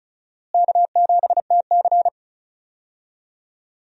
Morse code, heard as a keyed tone, about 34 wpm, 700 Hz. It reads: K7TC
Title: USA Call Signs - 4 Characters - Repeated 2x - 35wpm